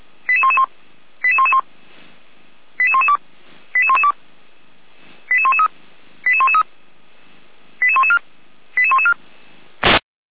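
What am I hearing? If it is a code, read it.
Five-tone selective call (ZVEI), heard as quick runs of five tones, tones of about 70 ms. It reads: 88111 88111 88112 88112 88113 88113 88114 88114
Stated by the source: ZVEI five-tone call received by SDR, from the audio server of feuerwehr-orlamuende.org (fire brigade Orlamünde).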